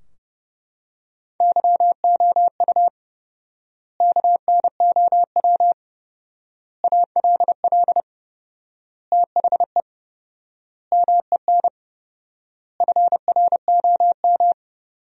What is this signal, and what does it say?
Morse code, heard as a keyed tone, about 30 wpm, 700 Hz. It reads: YOU KNOW ALL THE MEN FROM
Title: 4x Word-Spacing - Sentences from Top 300 Words - 30wpm